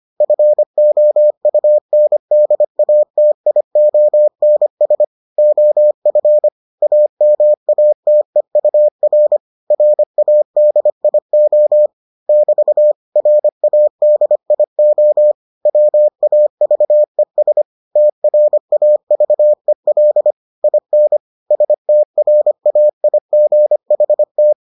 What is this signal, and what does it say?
Morse code, heard as a keyed tone, about 25 wpm, 600 Hz. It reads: FOUNDATIONS OF AMATEUR RADIO <BT> RADIO WAVES TRAVEL IN STRAIGHT